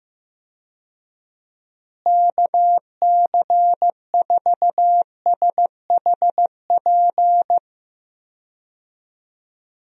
Morse code, heard as a keyed tone, about 15 wpm, 700 Hz. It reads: KC4SHP
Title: USA Call Signs - 6 Characters - Repeated 3x - 15wpm